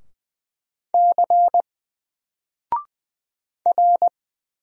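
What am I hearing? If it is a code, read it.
Morse code, heard as a keyed tone, about 20 wpm, 700 Hz. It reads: C R
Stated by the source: Single letter and number Morse drill at 20 wpm, from N4LCD.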